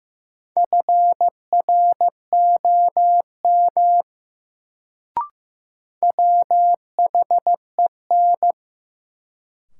Morse code, heard as a keyed tone, about 15 wpm, 700 Hz. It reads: FROM WHEN
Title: Sentences from Top 100 Words - New Word Drill - 15wpm